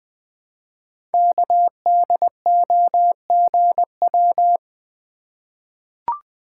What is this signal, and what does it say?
Morse code, heard as a keyed tone, about 20 wpm, 700 Hz. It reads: KDOGW
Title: Code Groups - 5 Letters - 20wpm